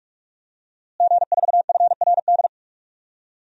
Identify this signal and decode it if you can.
Morse code, heard as a keyed tone, about 47 wpm, 700 Hz. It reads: G4FRD
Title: DXCC-100 Call Signs - 5 Characters - Repeated 2x - 45wpm